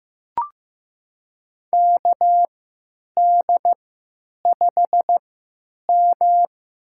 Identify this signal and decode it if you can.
Morse code, heard as a keyed tone, about 15 wpm, spaced as for 8 wpm, 700 Hz. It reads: KD5M
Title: USA Call Signs - 3x Character Spacing - 15wpm